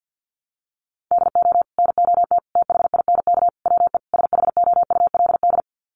Morse code, heard as a keyed tone, about 50 wpm, 700 Hz. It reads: DO NOT THINK WE SHOULD
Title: Sets of 5 Words Encore 50wpm